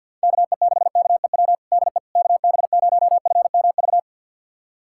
Morse code, heard as a keyed tone, about 47 wpm, 700 Hz. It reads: KE6KEW DE KB0FMV
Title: USA Callsign DE Callsign 50wpm